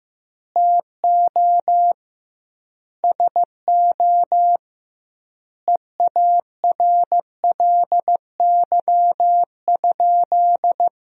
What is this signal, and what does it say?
Morse code, heard as a keyed tone, about 15 wpm, 700 Hz. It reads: TO SO EARLY?
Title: 2x Word-Spacing - Sentences from Top 400 Words - 15wpm